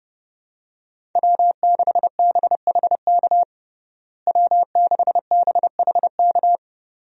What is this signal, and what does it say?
Morse code, heard as a keyed tone, about 30 wpm, 700 Hz. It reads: W6BHK W6BHK
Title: USA Call Signs - 5 Characters - Repeated 2x - 30wpm